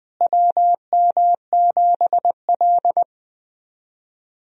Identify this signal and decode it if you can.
Morse code, heard as a keyed tone, about 20 wpm, 700 Hz. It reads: WM7L
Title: USA Call Signs - 4 Characters - Repeated 3x - 20wpm